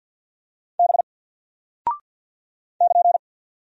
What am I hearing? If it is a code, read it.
Morse code, heard as a keyed tone, about 48 wpm, 700 Hz. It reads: B <KN>
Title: Single Character (Letter:Number:Punctuation:Prosign) Alt 50wpm